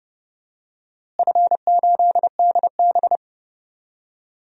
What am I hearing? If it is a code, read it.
Morse code, heard as a keyed tone, about 30 wpm, 700 Hz. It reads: F8DB
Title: DXCC-100 Call Signs - 4 Characters - Repeated 3x - 30wpm